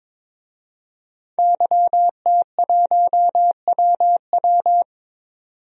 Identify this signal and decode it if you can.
Morse code, heard as a keyed tone, about 22 wpm, 700 Hz. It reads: YT1WW
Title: DXCC-100 Call Signs - 5 Characters - 22wpm